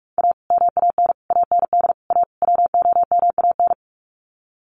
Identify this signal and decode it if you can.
Morse code, heard as a keyed tone, about 44 wpm, 700 Hz. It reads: A MAN AND A WOMAN
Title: Sets of 5 Words 45wpm